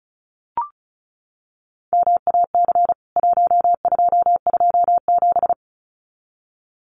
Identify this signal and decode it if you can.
Morse code, heard as a keyed tone, about 34 wpm, 700 Hz. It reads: MAC 1227
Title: CWT Exchange 35wpm